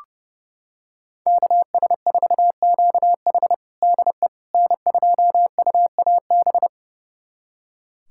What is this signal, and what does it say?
Morse code, heard as a keyed tone, about 32 wpm, 700 Hz. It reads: KS4QH DE N2UAB
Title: USA Callsign DE Callsign - 5 Characters - 30wpm